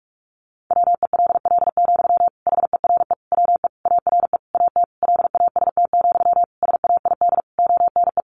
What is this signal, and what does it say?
Morse code, heard as a keyed tone, about 44 wpm, 700 Hz. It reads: WELL, HERE WE ARE AT LAST, SAID ONE
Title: Sentences from Top 300 Words 45wpm